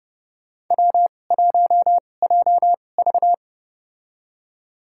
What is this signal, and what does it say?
Morse code, heard as a keyed tone, about 32 wpm, spaced as for 22 wpm, 700 Hz. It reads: W1JV
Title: USA Call Signs - 4 Characters - 2x Character Spacing - 30wpm